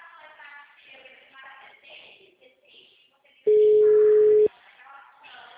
Ringback tone (Europe, 425 Hz).